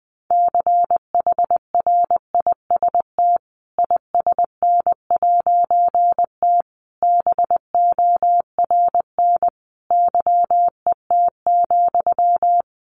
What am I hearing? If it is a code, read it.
Morse code, heard as a keyed tone, about 20 wpm, 700 Hz. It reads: CHRIST ISN'T BORN YET,
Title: A Sound Of Thunder 20wpm